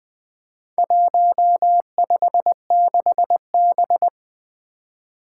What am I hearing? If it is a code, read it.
Morse code, heard as a keyed tone, about 20 wpm, 700 Hz. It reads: 156B
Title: Four Challenging Letter-Number Combinations 20wpm